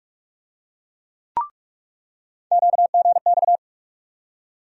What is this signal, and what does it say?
Morse code, heard as a keyed tone, about 43 wpm, 700 Hz. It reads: QGX
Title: USA Call Sign Postfixes - 3 Letters 45wpm